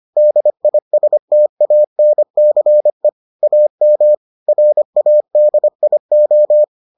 Morse code, heard as a keyed tone, about 25 wpm, 600 Hz. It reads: DISTANCE AM RADIO